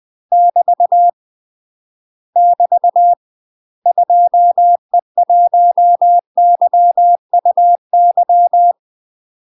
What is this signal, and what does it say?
Morse code, heard as a keyed tone, about 20 wpm, 700 Hz. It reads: <BT> <BT> 2E1YUY
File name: UK MASTER QSOs TRAINING / 20WPM / UK MASTER QSOs 0051